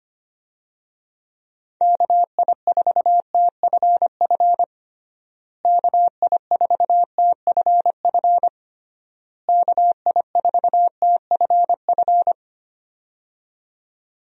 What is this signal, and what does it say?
Morse code, heard as a keyed tone, about 25 wpm, 700 Hz. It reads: KI4TFF KI4TFF KI4TFF
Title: USA Call Signs - 6 Characters - Repeated 3x - 25wpm